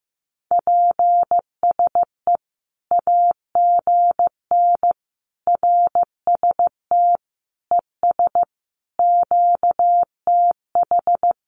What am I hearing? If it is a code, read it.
Morse code, heard as a keyed tone, about 15 wpm, 700 Hz. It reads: PSE AGN RST ES QTH
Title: Common QSO Phrases 15wpm